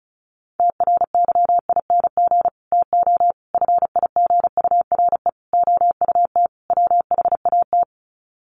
Morse code, heard as a keyed tone, about 34 wpm, 700 Hz. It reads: TRYING TO FIGURE OUT WHAT
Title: Sets of 5 Words 35wpm